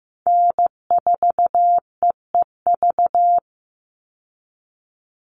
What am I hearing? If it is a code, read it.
Morse code, heard as a keyed tone, about 15 wpm, 700 Hz. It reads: N4EEV
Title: Call Signs Repeated 3x - 15wpm